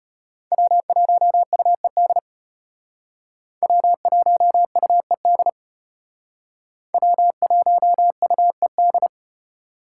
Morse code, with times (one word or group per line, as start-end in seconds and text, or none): 0.5-2.2 W1UED
3.6-5.5 W1UED
6.9-9.1 W1UED